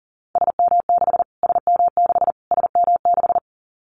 Morse code, heard as a keyed tone, about 40 wpm, 700 Hz. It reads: SM6 SM6 SM6
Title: Callsign Prefixes for Top 25 DXCC - Repeated 3x - 40wpm